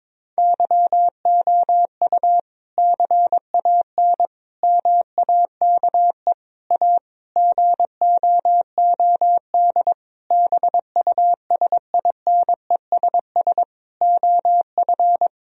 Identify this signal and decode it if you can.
Morse code, heard as a keyed tone, about 22 wpm, 700 Hz. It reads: YOU CAN MAKE A GOOD BUSINESS OF
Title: Sentences from Top 1000 Words - Review Part 4 - 22wpm